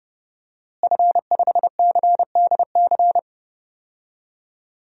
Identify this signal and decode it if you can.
Morse code, heard as a keyed tone, about 30 wpm, 700 Hz. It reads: F5CDC
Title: DXCC-100 Call Signs - 5 Characters - Repeated 2x - 30wpm